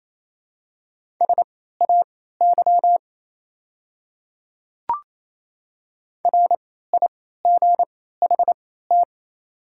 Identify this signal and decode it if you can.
Morse code, heard as a keyed tone, about 28 wpm, spaced as for 16 wpm, 700 Hz. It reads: SAY RIGHT
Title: 3x Character Spacing - Top 200 Words - 28wpm